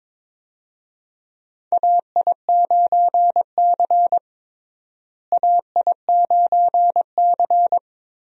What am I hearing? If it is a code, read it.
Morse code, heard as a keyed tone, about 21 wpm, 700 Hz. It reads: AI9C AI9C
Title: USA Call Signs - 4 Characters - Repeated 3x - 22wpm